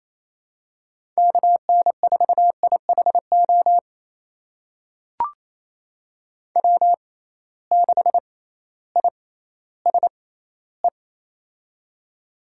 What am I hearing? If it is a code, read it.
Morse code, heard as a keyed tone, about 28 wpm, spaced as for 15 wpm, 700 Hz. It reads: KN4IHO W6ISE
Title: USA Call Signs - 6x Character Spacing - 28wpm